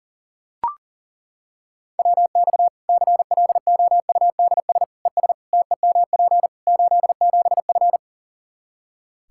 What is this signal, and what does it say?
Morse code, heard as a keyed tone, about 40 wpm, 700 Hz. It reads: WX CLOUDS ES TEMP 87F
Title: Practice QSO - Exchange 2 - Info - 40wpm